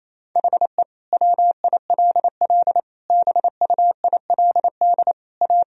Morse code, heard as a keyed tone, about 27 wpm, 700 Hz. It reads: HE WILL BUILD A
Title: Sentences from Top 1000 Words - Review Part 1 - 28wpm